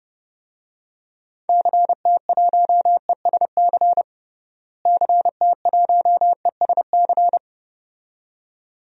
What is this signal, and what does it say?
Morse code, heard as a keyed tone, about 30 wpm, 700 Hz. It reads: CT1ESC CT1ESC
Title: DXCC-100 Call Signs - 6 Characters - Repeated 2x - 30wpm